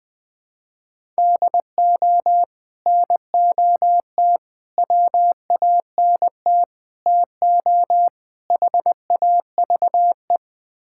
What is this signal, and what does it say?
Morse code, heard as a keyed tone, about 21 wpm, 700 Hz. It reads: DO NOT WANT TO HAVE
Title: Build A Phrase - 5-Word Phrases - 20wpm